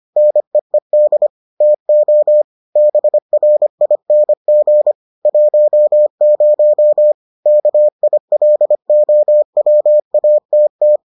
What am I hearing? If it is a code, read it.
Morse code, heard as a keyed tone, about 25 wpm, 600 Hz. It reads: NEED TO BRING 10 KILOWATT